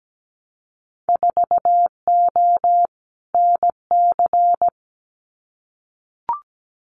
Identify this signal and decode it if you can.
Morse code, heard as a keyed tone, about 17 wpm, 700 Hz. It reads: 4O NC
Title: Winter Field Day Exchanges 17wpm